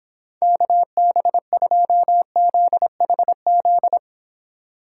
Morse code, heard as a keyed tone, about 30 wpm, 700 Hz. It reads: KB2ZHZ